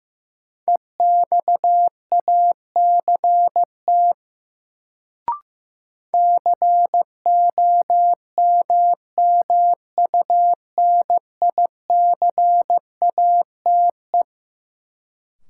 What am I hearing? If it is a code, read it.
Morse code, heard as a keyed tone, about 15 wpm, 700 Hz. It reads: EXACT COMMUNICATE
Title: Sentences from Top 1300 Words - New Word Drill - 15wpm